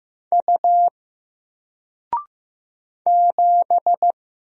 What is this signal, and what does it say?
Morse code, heard as a keyed tone, about 15 wpm, 700 Hz. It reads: U 7
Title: Single Letter-Number 15wpm